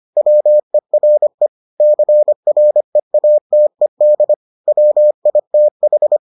Morse code, heard as a keyed tone, about 25 wpm, 600 Hz. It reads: WERE CREATED WITH